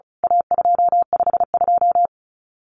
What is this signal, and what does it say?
Morse code, heard as a keyed tone, about 34 wpm, 700 Hz. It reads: A252
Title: CWT Exchange 35wpm